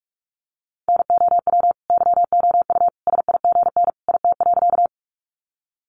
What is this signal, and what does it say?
Morse code, heard as a keyed tone, about 44 wpm, 700 Hz. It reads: NOW YOU SIGN IT.
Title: Sentences from Top 1000 Words 45wpm